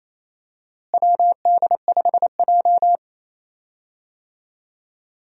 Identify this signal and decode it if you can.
Morse code, heard as a keyed tone, about 28 wpm, 700 Hz. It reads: WD5J
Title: USA Call Signs - 4 Characters - Repeated 2x - 28wpm